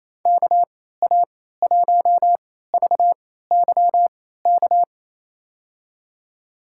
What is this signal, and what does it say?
Morse code, heard as a keyed tone, about 28 wpm, spaced as for 16 wpm, 700 Hz. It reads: KA1VYK
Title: USA Call Signs - 6 Characters - 3x Character Spacing - 28wpm